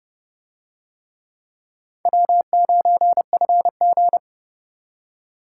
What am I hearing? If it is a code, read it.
Morse code, heard as a keyed tone, about 29 wpm, 700 Hz. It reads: W9FG